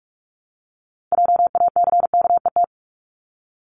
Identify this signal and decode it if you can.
Morse code, heard as a keyed tone, about 46 wpm, 690 Hz. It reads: JACKET
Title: Multi-letter Phonemes 45wpm